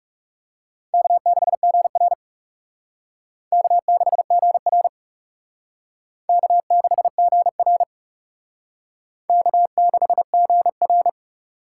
Morse code, heard as a keyed tone, about 35 wpm, 700 Hz. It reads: K6GR K6GR K6GR K6GR
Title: Speed Racing - USA 1x2 Call Signs 30wpm